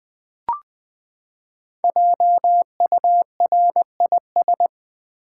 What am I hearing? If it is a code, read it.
Morse code, heard as a keyed tone, about 20 wpm, 700 Hz. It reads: JURIS
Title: Build A Word - 5 Letters At A Time - 12-Letter Words - 20wpm